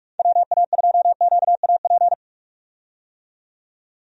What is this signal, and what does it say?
Morse code, heard as a keyed tone, about 43 wpm, 700 Hz. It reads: WA2QAP
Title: USA Call Signs - 6 Characters - 45wpm